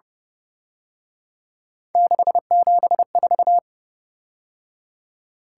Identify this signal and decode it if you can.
Morse code, heard as a keyed tone, about 29 wpm, 700 Hz. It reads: 674
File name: Speed-Racing - Three Letter-Number Combinations - 25wpm